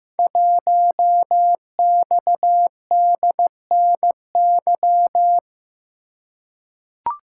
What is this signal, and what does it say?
Morse code, heard as a keyed tone, about 15 wpm, 700 Hz. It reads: 1XDNY
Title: Code Groups - 5 Letter:Numbers - 15wpm